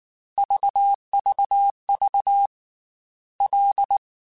Morse code, heard as a keyed tone, about 19 wpm, 800 Hz. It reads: VVV L
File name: F and L / F and L mp3 / F and L Prefixes 3x2 Twice @18